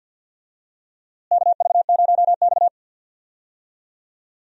Morse code, heard as a keyed tone, about 52 wpm, 700 Hz. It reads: KV0X
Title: USA Call Signs - 4 Characters - Repeated 2x - 50wpm